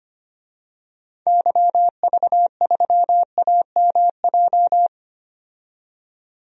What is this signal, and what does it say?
Morse code, heard as a keyed tone, about 25 wpm, 700 Hz. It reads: YV3AMJ
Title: DXCC-100 Call Signs - 6 Characters - 25wpm